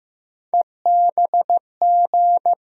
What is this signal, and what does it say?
Morse code, heard as a keyed tone, about 15 wpm, 700 Hz. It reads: EBG
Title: USA Call Sign Postfixes - 3 Letters 15wpm